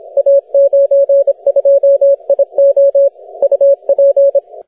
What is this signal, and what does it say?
Morse code, heard as a keyed tone, about 25 wpm, 560 Hz. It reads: A92IO UP
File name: wp / a92io 15m cw 11 dec 2010 0918z